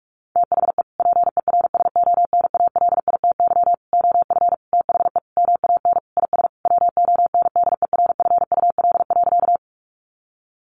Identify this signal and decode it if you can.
Morse code, heard as a keyed tone, about 44 wpm, 700 Hz. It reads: THE PERSONALITY OF THE MAN IS WONDERFUL.
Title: Sentences from Top 1400 Words 45wpm